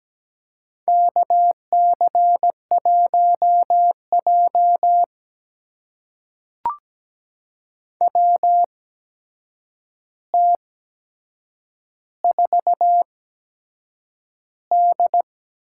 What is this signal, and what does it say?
Morse code, heard as a keyed tone, about 17 wpm, spaced as for 7 wpm, 700 Hz. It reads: KC1J WT4D
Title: USA Call Signs - 4 Characters - 8x Character Spacing - 17wpm